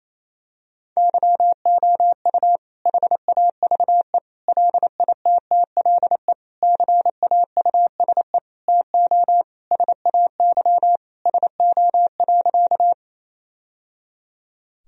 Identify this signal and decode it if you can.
Morse code, heard as a keyed tone, about 28 wpm, 700 Hz. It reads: YOU HAVE LITTLE CAUSE TO SAY SO.
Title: Sentences from Top 1000 Words - Review Part 1 - 28wpm